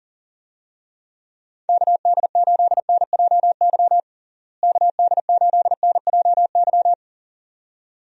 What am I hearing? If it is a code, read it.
Morse code, heard as a keyed tone, about 40 wpm, 700 Hz. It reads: KD8NJY KD8NJY
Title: USA Call Signs - 6 Characters - Repeated 2x - 40wpm